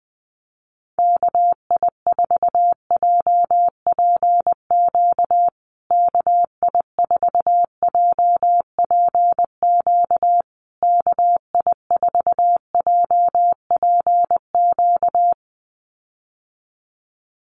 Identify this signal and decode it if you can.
Morse code, heard as a keyed tone, about 20 wpm, 700 Hz. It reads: KI4JPQ KI4JPQ KI4JPQ
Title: USA Call Signs Repeated 3x 20wpm